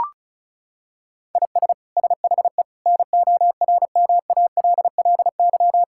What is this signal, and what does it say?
Morse code, heard as a keyed tone, about 36 wpm, 700 Hz. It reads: IS SHE NORMALLY
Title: Sentences from Top 1000 Words - Review Part 6 - 35wpm